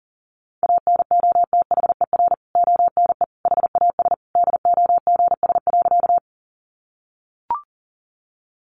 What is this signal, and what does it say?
Morse code, heard as a keyed tone, about 40 wpm, 700 Hz. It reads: ANOTHER ONE HAS DOGS.
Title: Additional Easy Phrases 40wpm